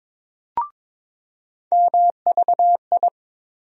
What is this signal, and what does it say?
Morse code, heard as a keyed tone, about 22 wpm, 700 Hz. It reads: MVI
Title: USA Call Sign Postfixes - 3 Letters 22wpm